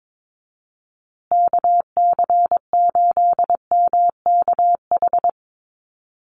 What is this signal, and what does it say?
Morse code, heard as a keyed tone, about 22 wpm, 700 Hz. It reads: KC8MKH
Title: USA Call Signs Repeated 2x 22wpm